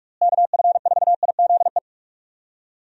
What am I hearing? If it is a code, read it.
Morse code, heard as a keyed tone, about 42 wpm, 700 Hz. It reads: KF4IZE
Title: USA Call Signs - 6 Characters - Repeated 2x - 45wpm